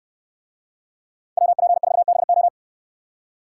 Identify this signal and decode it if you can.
Morse code, heard as a keyed tone, about 69 wpm, 700 Hz. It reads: WP4BO